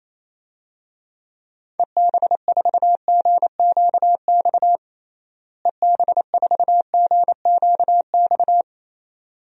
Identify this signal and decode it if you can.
Morse code, heard as a keyed tone, about 27 wpm, 700 Hz. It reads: EB4GQX EB4GQX
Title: DXCC-100 Call Signs - 6 Characters - Repeated 2x - 28wpm